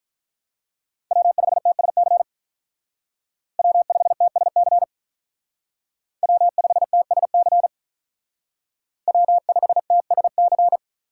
Morse code, heard as a keyed tone, about 45 wpm, 700 Hz. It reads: W5TSC W5TSC W5TSC W5TSC